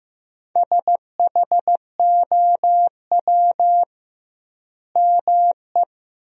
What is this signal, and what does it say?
Morse code, heard as a keyed tone, about 15 wpm, 700 Hz. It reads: SHOW ME
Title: 2x Word-Spacing - Sentences from Top 200 Words - 15wpm